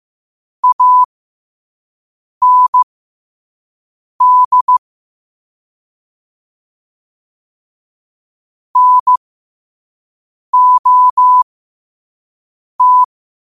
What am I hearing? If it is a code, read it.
Morse code, heard as a keyed tone, about 15 wpm, spaced as for 5 wpm, 1000 Hz. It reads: AND NOT
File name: qotdfiles / 5wpm